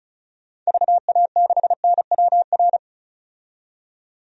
Morse code, heard as a keyed tone, about 34 wpm, 700 Hz. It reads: VA6NWR
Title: DXCC-100 Call Signs - 6 Characters - 35wpm